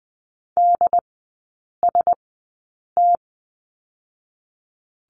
Morse code, heard as a keyed tone, about 20 wpm, spaced as for 8 wpm, 700 Hz. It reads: DST
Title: 067 Lesson 12e - Letter d - Three Characters 20wpm